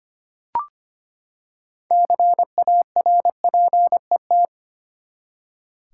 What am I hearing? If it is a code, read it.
Morse code, heard as a keyed tone, about 25 wpm, 700 Hz. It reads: CARPET
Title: Six Letter Words - 25wpm